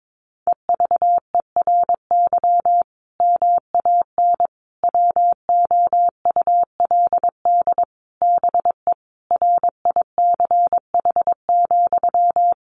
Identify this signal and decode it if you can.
Morse code, heard as a keyed tone, about 22 wpm, 700 Hz. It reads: EVERY MAN WOULD BE RICH,
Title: Sentences from Top 1500 Words 22wpm